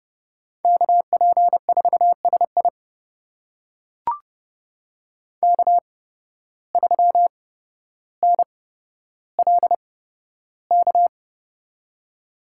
Morse code, heard as a keyed tone, about 30 wpm, spaced as for 13 wpm, 700 Hz. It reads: KP4SI K3NLK
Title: USA Call Signs - 5 Characters - 8x Character Spacing - 30wpm